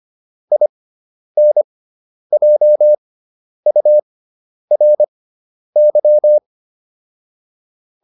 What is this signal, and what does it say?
Morse code, heard as a keyed tone, about 25 wpm, spaced as for 10 wpm, 600 Hz. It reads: INJURY